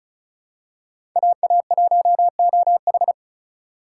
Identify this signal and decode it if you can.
Morse code, heard as a keyed tone, about 36 wpm, 700 Hz. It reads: AA1OH
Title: USA Call Signs - 5 Characters - 3x Character Spacing - 35wpm